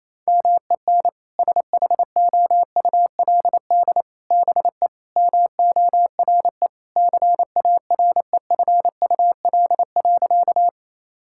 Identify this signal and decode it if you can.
Morse code, heard as a keyed tone, about 28 wpm, 700 Hz. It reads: MEN SHOULD BE MORE CAREFUL.
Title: Sentences from Top 1600 Words 28wpm